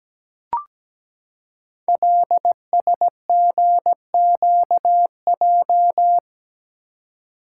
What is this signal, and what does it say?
Morse code, heard as a keyed tone, about 17 wpm, 700 Hz. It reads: LSGQJ
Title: Code Groups - 5 Letter:Numbers - 17wpm